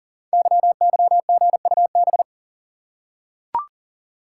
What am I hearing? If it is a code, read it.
Morse code, heard as a keyed tone, about 42 wpm, 690 Hz. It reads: YYGUB
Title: Code Groups - 5 Letters - 40wpm